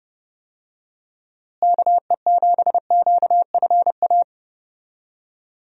Morse code, heard as a keyed tone, about 32 wpm, 700 Hz. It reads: KE7QFA
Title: USA Call Signs - 6 Characters - Repeated 2x - 30wpm